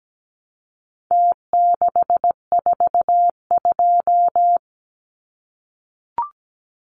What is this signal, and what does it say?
Morse code, heard as a keyed tone, about 17 wpm, 700 Hz. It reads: T642
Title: Numbers - Contest Style - 4-Digit 17wpm